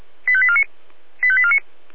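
Five-tone selective call (ZVEI), heard as quick runs of five tones, tones of about 70 ms. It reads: 85849 85849
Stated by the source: ZVEI five-tone call 85840 received on an SDR.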